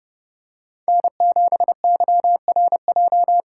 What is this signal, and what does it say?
Morse code, heard as a keyed tone, about 31 wpm, 700 Hz. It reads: N7YRJ
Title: USA Call Signs - 5 Characters - 4x Character Spacing - 30wpm